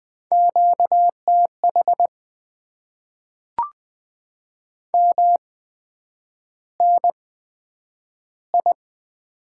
Morse code, heard as a keyed tone, about 20 wpm, spaced as for 9 wpm, 700 Hz. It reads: QTH MNI
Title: Top 90 QSO Elements - 8x Character Spacing - 20wpm